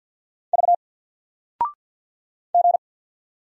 Morse code, heard as a keyed tone, about 48 wpm, 700 Hz. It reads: V G
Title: Single Character (Letter:Number:Punctuation:Prosign) 50wpm